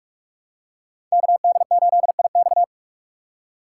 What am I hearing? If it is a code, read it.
Morse code, heard as a keyed tone, about 43 wpm, 700 Hz. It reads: KD8IX
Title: USA Call Signs - 5 Characters - 3x Character Spacing - 45wpm